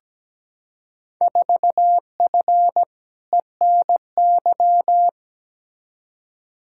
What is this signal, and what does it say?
Morse code, heard as a keyed tone, about 17 wpm, 700 Hz. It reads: 4F ENY
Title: Field Day Exchanges - 17wpm